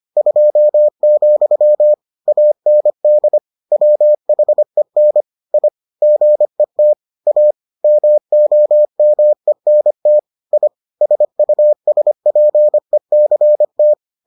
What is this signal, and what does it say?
Morse code, heard as a keyed tone, about 25 wpm, 600 Hz. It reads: 2, AND WHEN I GET A MOMENT I SUSPECT